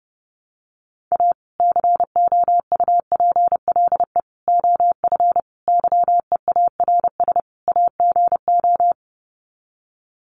Morse code, heard as a keyed tone, about 30 wpm, 700 Hz. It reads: A COUPLE OF YEARS AGO
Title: Sets of 5 Words 30wpm